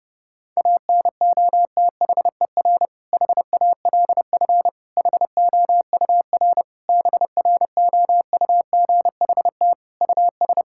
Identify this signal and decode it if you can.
Morse code, heard as a keyed tone, about 32 wpm, 700 Hz. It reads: ANOTHER HALF HOUR BROUGHT US
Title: Sentences from Top 1600 Words 30wpm